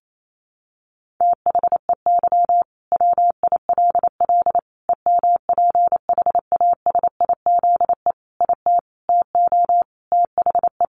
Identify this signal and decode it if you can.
Morse code, heard as a keyed tone, about 28 wpm, 700 Hz. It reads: THEY WILL EMPHASIZE IT TO THE
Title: Sentences from Top 1500 Words 28wpm